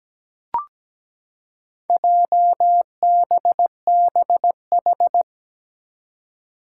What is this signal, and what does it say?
Morse code, heard as a keyed tone, about 17 wpm, 700 Hz. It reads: JBBH
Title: Four Challenging Letter-Number Combinations 17wpm